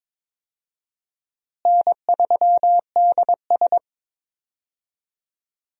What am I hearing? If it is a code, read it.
Morse code, heard as a keyed tone, about 22 wpm, 700 Hz. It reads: N3DS
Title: USA Call Signs - 4 Characters - Repeated 2x - 22wpm